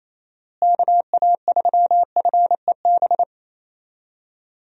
Morse code, heard as a keyed tone, about 27 wpm, 700 Hz. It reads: KA3FEB
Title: USA Call Signs - 6 Characters - Repeated 3x - 28wpm